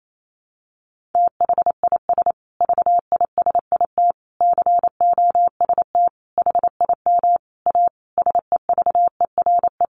Morse code, heard as a keyed tone, about 28 wpm, 700 Hz. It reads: THIS VISIT COST HIM A SEVERE
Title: Sentences from Top 1400 Words 28wpm